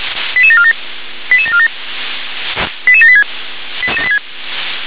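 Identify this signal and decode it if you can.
Five-tone selective call (ZVEI), heard as quick runs of five tones, tones of about 70 ms. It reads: 88747 88747 88786
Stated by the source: Received on SDR; ZVEI five-tone selective call.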